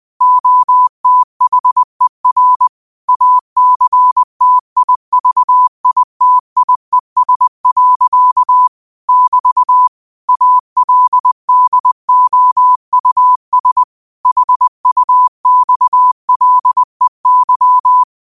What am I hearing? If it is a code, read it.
Morse code, heard as a keyed tone, about 20 wpm, 1000 Hz. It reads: OTHER ACTIVITIES. <BT> ALDOUS HUXLEY
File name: qotdfiles / 20wpm